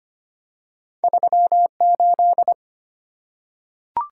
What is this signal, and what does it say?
Morse code, heard as a keyed tone, about 25 wpm, 700 Hz. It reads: 38
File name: Two Letter-Number Combinations - 25wpm